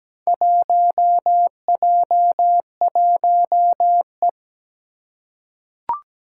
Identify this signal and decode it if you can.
Morse code, heard as a keyed tone, about 17 wpm, 700 Hz. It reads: 1J1E
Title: Four Challenging Letter-Number Combinations 17wpm